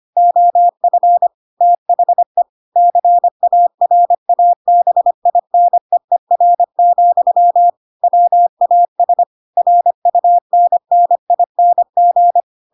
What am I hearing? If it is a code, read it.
Morse code, heard as a keyed tone, about 25 wpm, 700 Hz. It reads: OF THE CARABINEER, WAS RUNNING
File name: Pinocchio / Pinocchio ch4 25 25 ik1hge